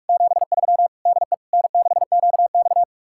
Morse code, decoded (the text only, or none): 73 DE N6QX